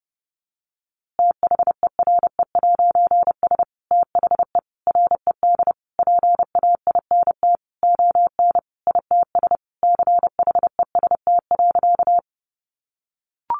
Morse code, heard as a keyed tone, about 30 wpm, 700 Hz. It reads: THERE'S THE RED PAINT ON ITS CHEST.
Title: A Sound Of Thunder 30wpm